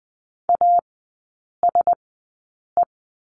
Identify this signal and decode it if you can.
Morse code, heard as a keyed tone, about 20 wpm, spaced as for 8 wpm, 700 Hz. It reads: ASE
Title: 087 Lesson 15e - Number 5 - Three Characters 20wpm